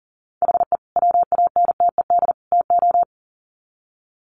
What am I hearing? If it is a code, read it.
Morse code, heard as a keyed tone, about 40 wpm, 700 Hz. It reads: HE WANTED TO
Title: Sets of 3 Words 40wpm